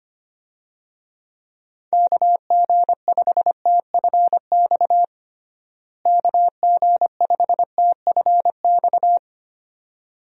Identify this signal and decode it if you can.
Morse code, heard as a keyed tone, about 25 wpm, 700 Hz. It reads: KG5TFX KG5TFX
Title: USA Call Signs - 6 Characters - Repeated 2x - 25wpm